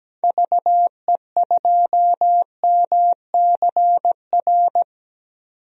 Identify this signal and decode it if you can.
Morse code, heard as a keyed tone, about 17 wpm, 700 Hz. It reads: VE2MCR
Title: DXCC-100 Call Signs - 6 Characters - 17wpm